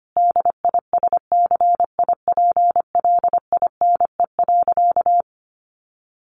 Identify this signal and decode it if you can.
Morse code, heard as a keyed tone, about 25 wpm, 700 Hz. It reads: DISCIPLINE.
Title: Sentences from Top 900 Words 25wpm